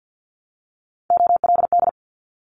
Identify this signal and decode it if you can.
Morse code, heard as a keyed tone, about 50 wpm, 700 Hz. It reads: OLD
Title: Top 500 Words Repeated 50wpm